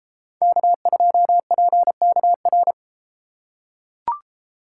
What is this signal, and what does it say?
Morse code, heard as a keyed tone, about 31 wpm, 700 Hz. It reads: K2PKR